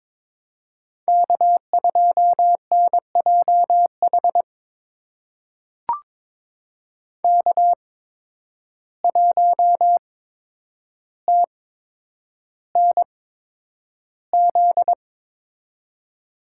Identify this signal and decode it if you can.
Morse code, heard as a keyed tone, about 22 wpm, spaced as for 9 wpm, 700 Hz. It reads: K2NJH K1TNZ
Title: USA Call Signs - 5 Characters - 8x Character Spacing - 22wpm